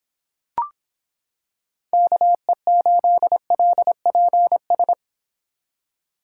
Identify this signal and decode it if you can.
Morse code, heard as a keyed tone, about 26 wpm, 700 Hz. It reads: KE8LPS